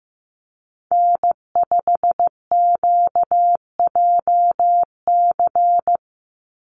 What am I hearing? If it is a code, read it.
Morse code, heard as a keyed tone, about 15 wpm, 700 Hz. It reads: N5QJC
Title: USA Call Signs 15wpm